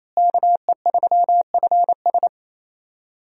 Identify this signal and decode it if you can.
Morse code, heard as a keyed tone, about 28 wpm, 700 Hz. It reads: KE3FS
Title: USA Call Signs - 5 Characters - Repeated 2x - 28wpm